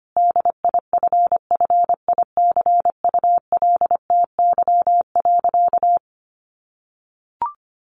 Morse code, heard as a keyed tone, about 25 wpm, 700 Hz. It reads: DIFFICULTY.
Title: Sentences from Top 1500 Words 25wpm